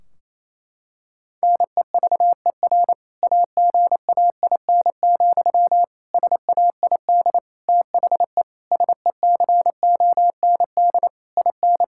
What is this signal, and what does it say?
Morse code, heard as a keyed tone, about 28 wpm, 700 Hz. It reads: NEVER AGAIN, SAID THE SECOND IN
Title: Sentences from Top 1600 Words 28wpm